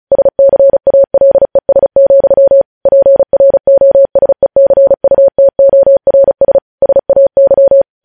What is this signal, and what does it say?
Morse code, heard as a keyed tone, about 36 wpm, 550 Hz. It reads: SCALES, PROSECUTORS SAY
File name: podcast / News headlines-1x-35x35